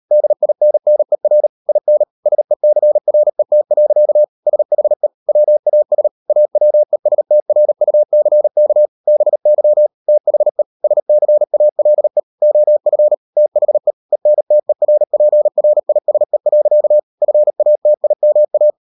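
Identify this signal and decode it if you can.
Morse code, heard as a keyed tone, about 38 wpm, 600 Hz. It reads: DINNER IN SECRET. SHE WAS AWESTRUCK BY THE SCALE OF THE ENTERPRISE. FATIMA
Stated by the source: Morse code practice at 38 WPM, sending Ken Follett's novel Never in English, from CW med Gnister.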